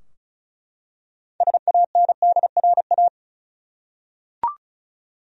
Morse code, heard as a keyed tone, about 36 wpm, 700 Hz. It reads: SANDRA
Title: USA Common Names - 35wpm